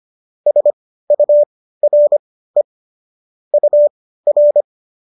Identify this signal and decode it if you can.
Morse code, heard as a keyed tone, about 25 wpm, spaced as for 15 wpm, 590 Hz. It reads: SURE UR